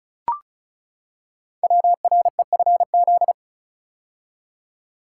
Morse code, heard as a keyed tone, about 36 wpm, 700 Hz. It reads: WREFZ
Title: Code Groups - 5 Letter:Numbers - 35wpm